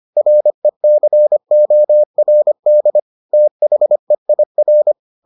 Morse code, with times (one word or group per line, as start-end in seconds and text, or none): 0.2-3.0 RECORD
3.3-4.9 THEIR